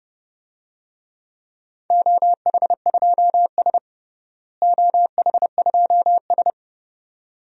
Morse code, heard as a keyed tone, about 32 wpm, 700 Hz. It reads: OH2S OH2S
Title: DXCC-100 Call Signs - 4 Characters - Repeated 2x - 30wpm